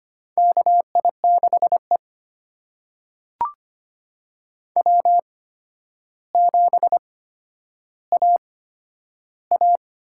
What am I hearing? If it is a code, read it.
Morse code, heard as a keyed tone, about 25 wpm, spaced as for 11 wpm, 700 Hz. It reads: KI6E W7AA
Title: USA Call Signs - 8x Character Spacing - 25wpm